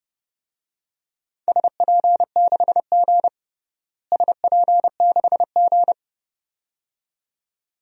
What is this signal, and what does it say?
Morse code, heard as a keyed tone, about 30 wpm, 700 Hz. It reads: SP6G SP6G
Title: DXCC-100 Call Signs - 4 Characters - Repeated 2x - 30wpm